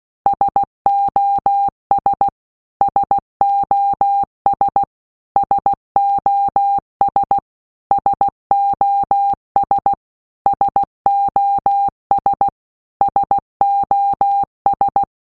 Morse code, heard as a keyed tone, about 16 wpm, 800 Hz. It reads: SOS SOS SOS SOS SOS SOS
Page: SOS sound in Morse Code (Slow)